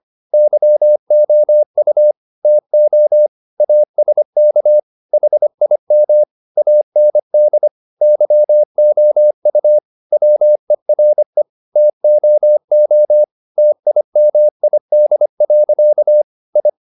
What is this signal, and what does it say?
Morse code, heard as a keyed tone, about 25 wpm, 600 Hz. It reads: YOU TO ASK HIM AND YOU WERE TOO TIMID. I
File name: the old man and the sea part 1 25wpm